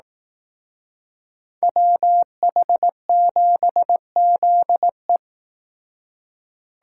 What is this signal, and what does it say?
Morse code, heard as a keyed tone, about 18 wpm, 700 Hz. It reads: WH7ZE
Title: Speed Racing - USA 2x2 Call Signs 15wpm